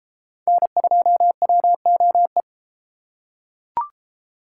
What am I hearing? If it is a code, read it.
Morse code, heard as a keyed tone, about 32 wpm, 700 Hz. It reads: N2WOE